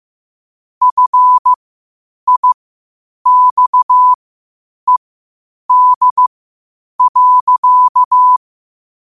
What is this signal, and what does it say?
Morse code, heard as a keyed tone, about 15 wpm, spaced as for 8 wpm, 1000 Hz. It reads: FIXED.
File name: qotdfiles / 8wpm